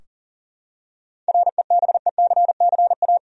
Morse code, heard as a keyed tone, about 40 wpm, 700 Hz. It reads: REBECCA
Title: USA Common Names - 40wpm